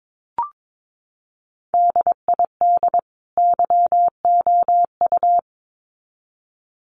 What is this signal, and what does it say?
Morse code, heard as a keyed tone, about 22 wpm, 700 Hz. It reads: DID YOU
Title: Sets of 2 Words 22wpm